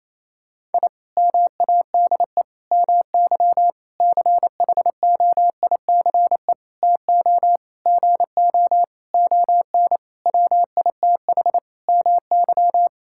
Morse code, heard as a keyed tone, about 28 wpm, 700 Hz. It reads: I MADE MY CHOICE TO GO ON WITH MY
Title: Sentences from Top 1000 Words - Review Part 6 - 28wpm